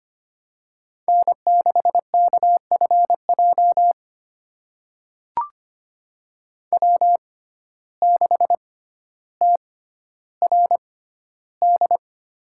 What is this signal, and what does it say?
Morse code, heard as a keyed tone, about 25 wpm, spaced as for 13 wpm, 700 Hz. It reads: N6KFJ W6TRD
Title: USA Call Signs - 5 Characters - 6x Character Spacing - 25wpm